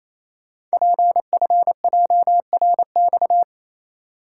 Morse code, heard as a keyed tone, about 28 wpm, 700 Hz. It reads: PFJRX
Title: Code Groups - 5 Letters - 28wpm